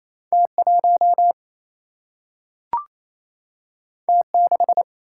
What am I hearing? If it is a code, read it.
Morse code, heard as a keyed tone, about 28 wpm, 700 Hz. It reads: T1 T6
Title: Single-Letter Single-Number 28wpm